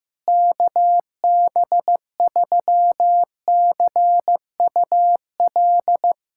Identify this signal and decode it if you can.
Morse code, heard as a keyed tone, about 15 wpm, 700 Hz. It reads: KB3CUL
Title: USA Callsign DE Callsign 15wpm